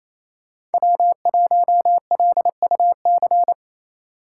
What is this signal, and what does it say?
Morse code, heard as a keyed tone, about 28 wpm, 700 Hz. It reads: W1LUC
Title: USA Call Signs - 3x Character Spacing - 28wpm